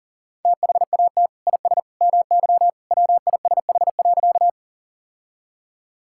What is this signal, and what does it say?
Morse code, heard as a keyed tone, about 37 wpm, 700 Hz. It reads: THAT IS MY WISH.
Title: Sentences from Top 1600 Words 40wpm